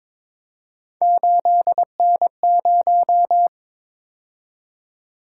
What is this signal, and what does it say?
Morse code, heard as a keyed tone, about 22 wpm, 700 Hz. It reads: 8N0
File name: practice / 22wpm / Three Letter-Number Combinations - 22wpm